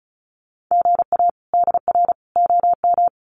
Morse code, heard as a keyed tone, about 34 wpm, 700 Hz. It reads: GA DR OM
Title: Common QSO Phrases 35wpm